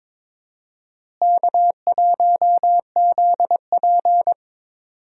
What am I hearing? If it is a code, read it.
Morse code, heard as a keyed tone, about 21 wpm, 700 Hz. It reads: K1ZP
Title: USA Call Signs - 4 Characters - Repeated 3x - 22wpm